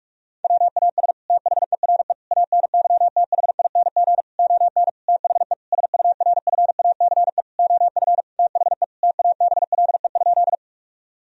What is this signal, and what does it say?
Morse code, heard as a keyed tone, about 46 wpm, 700 Hz. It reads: WAS THERE ANYTHING ON THE SURFACE OF THE TABLE?
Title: Sentences from Top 1000 Words - Review Part 3 - 45wpm